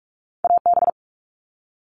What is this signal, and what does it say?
Morse code, heard as a keyed tone, about 46 wpm, 700 Hz. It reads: AB
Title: Canadian Provinces and Territories - Repeated 3x - 45wpm